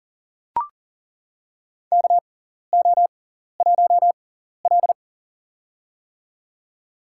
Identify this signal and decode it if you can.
Morse code, heard as a keyed tone, about 40 wpm, spaced as for 14 wpm, 700 Hz. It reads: KO1L
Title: USA Call Signs - 4 Characters - 6x Character Spacing - 40wpm